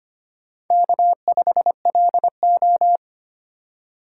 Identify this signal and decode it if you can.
Morse code, heard as a keyed tone, about 25 wpm, 700 Hz. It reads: K5LO
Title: USA Call Signs - 4 Characters - 4x Character Spacing - 25wpm